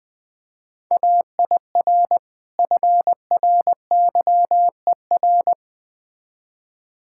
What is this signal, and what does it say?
Morse code, heard as a keyed tone, about 21 wpm, 700 Hz. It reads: AIR FRYER
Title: Easy Words - Speed-Racing with Hints - 20wpm